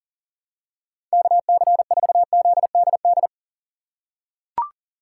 Morse code, heard as a keyed tone, about 40 wpm, 700 Hz. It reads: KC4ZDD
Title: USA Call Signs - 6 Characters - 2x Character Spacing - 40wpm